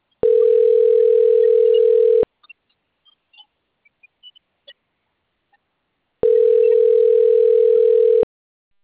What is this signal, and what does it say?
Ringback tone (North America), 440 + 480 Hz.